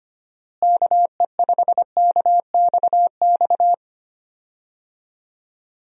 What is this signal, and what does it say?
Morse code, heard as a keyed tone, about 25 wpm, 700 Hz. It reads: KE5KXX
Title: USA Call Signs - 6 Characters - Repeated 2x - 25wpm